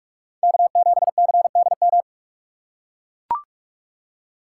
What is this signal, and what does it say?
Morse code, heard as a keyed tone, about 44 wpm, 700 Hz. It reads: K7CDM